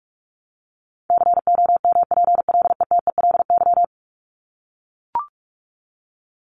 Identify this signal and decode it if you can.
Morse code, heard as a keyed tone, about 44 wpm, 700 Hz. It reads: COMPLETELY
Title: LY Suffix - Top 100 Words with Base Words 45wpm